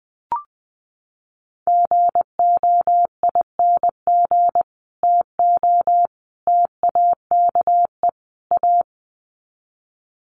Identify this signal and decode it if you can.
Morse code, heard as a keyed tone, about 19 wpm, 700 Hz. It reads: GOING TO TAKE A
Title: Sets of 4 Words 20wpm